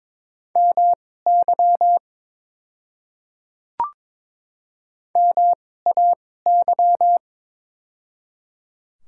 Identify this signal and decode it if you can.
Morse code, heard as a keyed tone, about 22 wpm, spaced as for 16 wpm, 700 Hz. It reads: MY MAY
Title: 2x Character Spacing - Top 100 Words - 22wpm